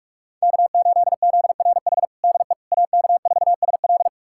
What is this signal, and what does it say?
Morse code, heard as a keyed tone, about 46 wpm, 700 Hz. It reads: K8ZRH DE AK4SL
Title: USA Callsign DE Callsign - 5 Characters - 45wpm